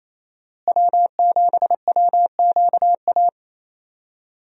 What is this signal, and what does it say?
Morse code, heard as a keyed tone, about 28 wpm, 700 Hz. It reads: W7WQA
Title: USA Call Signs - 5 Characters - Repeated 3x - 28wpm